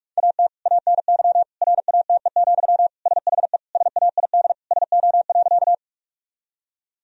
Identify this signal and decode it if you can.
Morse code, heard as a keyed tone, about 42 wpm, 700 Hz. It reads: AT ANY RATE, SHE SAID SO.
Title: Sentences from Top 1000 Words - Review Part 4 - 45wpm